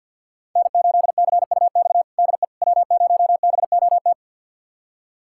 Morse code, heard as a keyed tone, about 46 wpm, 700 Hz. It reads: N8CUX DE W0BOT
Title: USA Callsign DE Callsign - 5 Characters - 50wpm